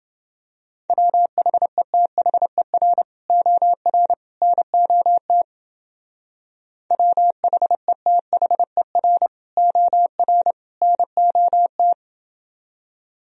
Morse code, heard as a keyed tone, about 28 wpm, 700 Hz. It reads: WHETHER OR NOT WHETHER OR NOT